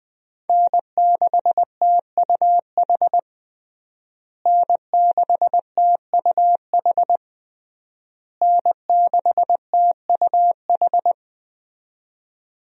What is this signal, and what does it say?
Morse code, heard as a keyed tone, about 21 wpm, 700 Hz. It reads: N6TUH N6TUH N6TUH
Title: USA Call Signs - 5 Characters - Repeated 3x - 20wpm